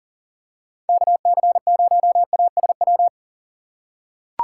Morse code, heard as a keyed tone, about 38 wpm, 700 Hz. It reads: KC0ASW
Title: USA Call Signs - 6 Characters - 4x Character Spacing - 40wpm